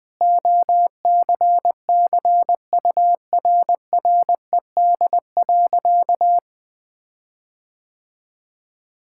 Morse code, heard as a keyed tone, about 20 wpm, 700 Hz. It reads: OCCURRED.
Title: Sentences with 6-Word Phrases - Spoken Beginning - 20wpm